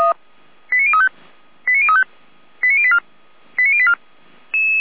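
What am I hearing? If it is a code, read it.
Five-tone selective call (ZVEI), heard as quick runs of five tones, tones of about 70 ms. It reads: 89035 89035 89084 89084
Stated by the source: ZVEI five-tone selective call received by SDR.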